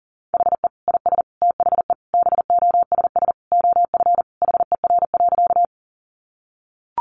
Morse code, heard as a keyed tone, about 40 wpm, 700 Hz. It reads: HE IS THE BOSS OF HER.
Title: Sentences from Top 700 Words 40wpm